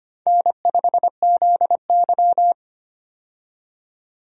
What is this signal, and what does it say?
Morse code, heard as a keyed tone, about 25 wpm, 700 Hz. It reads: N5ZY
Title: USA Call Signs - 4 Characters - 25wpm